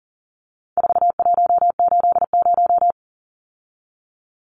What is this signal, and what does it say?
Morse code, heard as a keyed tone, about 40 wpm, 700 Hz. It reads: 4180
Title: Numbers - 4-Digit 40wpm